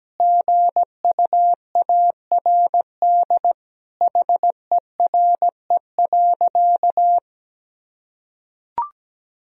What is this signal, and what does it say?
Morse code, heard as a keyed tone, about 17 wpm, 700 Hz. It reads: GUARD HERE.
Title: Sentences from Top 1000 Words - Review Part 4 - 17wpm